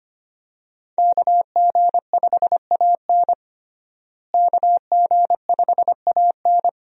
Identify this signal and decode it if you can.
Morse code, heard as a keyed tone, about 25 wpm, 700 Hz. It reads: KG5AN KG5AN
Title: USA Call Signs - 5 Characters - Repeated 2x - 25wpm